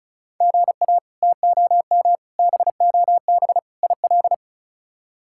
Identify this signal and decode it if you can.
Morse code, heard as a keyed tone, about 36 wpm, 700 Hz. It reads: GA TOM BOB IL
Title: K1USN SST QSO Practice - 35wpm